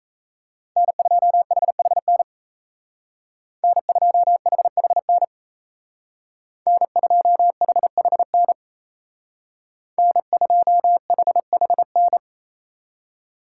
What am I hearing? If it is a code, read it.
Morse code, heard as a keyed tone, about 33 wpm, 700 Hz. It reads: N2HHN N2HHN N2HHN N2HHN
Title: Speed Racing - USA 1x3 Call Signs 28wpm